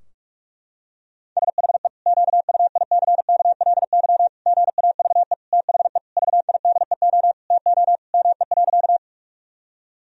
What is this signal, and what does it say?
Morse code, heard as a keyed tone, about 41 wpm, 700 Hz. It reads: SHE QUICKLY GAVE THE VIDEO TO ME.
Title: Sentences from Top 1000 Words - Review Part 4 - 45wpm